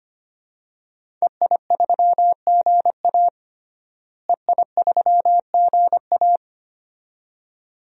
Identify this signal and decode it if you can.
Morse code, heard as a keyed tone, about 25 wpm, 700 Hz. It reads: EI3GA EI3GA
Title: DXCC-100 Call Signs - 5 Characters - Repeated 2x - 25wpm